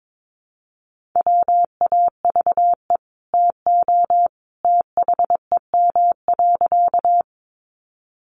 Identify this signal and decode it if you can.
Morse code, heard as a keyed tone, about 22 wpm, 700 Hz. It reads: WAVE TO THEM.
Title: Sentences from Top 1400 Words 22wpm